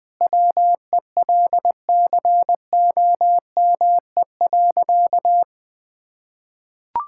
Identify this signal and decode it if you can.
Morse code, heard as a keyed tone, about 21 wpm, 700 Hz. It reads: WELCOME.
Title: Sentences with 7-Word Phrases - Spoken Beginning - 20wpm